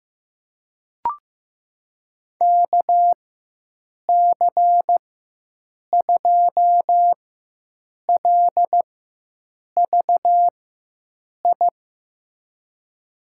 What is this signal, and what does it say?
Morse code, heard as a keyed tone, about 16 wpm, spaced as for 7 wpm, 700 Hz. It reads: KC2LVI
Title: USA Call Signs - 6 Characters - 4x Character Spacing - 15wpm